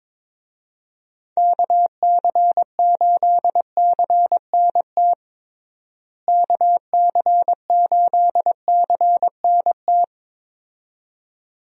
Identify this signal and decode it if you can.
Morse code, heard as a keyed tone, about 22 wpm, 700 Hz. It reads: KC8CNT KC8CNT
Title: USA Call Signs - 6 Characters - Repeated 2x - 22wpm